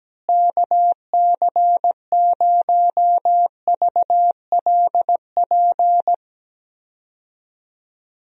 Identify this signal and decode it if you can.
Morse code, heard as a keyed tone, about 17 wpm, 700 Hz. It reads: KC0VLP
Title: USA Call Signs - 6 Characters - Repeated 2x - 17wpm